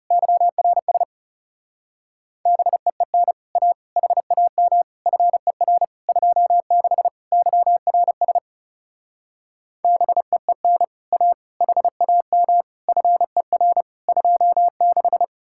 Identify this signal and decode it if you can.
Morse code, heard as a keyed tone, about 36 wpm, 700 Hz. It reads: YRS BEEN A HAM FER 26 YRS BEEN A HAM FER 26
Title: Speed-Racing Common QSO Phrases - 30wpm